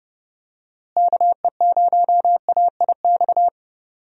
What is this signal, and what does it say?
Morse code, heard as a keyed tone, about 30 wpm, 700 Hz. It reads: KE0AIX
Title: USA Call Signs - 6 Characters - Repeated 2x - 30wpm